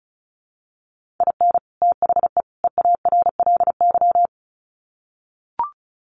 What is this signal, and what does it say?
Morse code, heard as a keyed tone, about 34 wpm, 700 Hz. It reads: IN THE EARLY
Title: Sets of 3 Words 35wpm